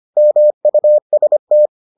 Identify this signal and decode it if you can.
Morse code, heard as a keyed tone, about 25 wpm, 600 Hz. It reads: MUST